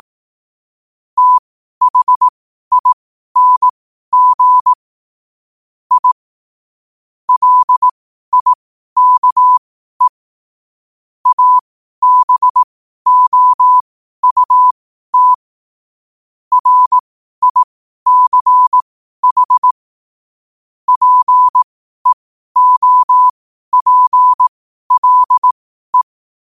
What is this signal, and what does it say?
Morse code, heard as a keyed tone, about 18 wpm, spaced as for 12 wpm, 1000 Hz. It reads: THING I LIKE ABOUT RICH PEOPLE
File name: qotdfiles / 12wpm